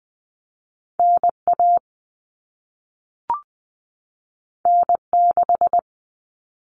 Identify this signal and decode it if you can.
Morse code, heard as a keyed tone, about 20 wpm, 700 Hz. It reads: NA N6
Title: Common Cut Numbers - 2-Digit 20wpm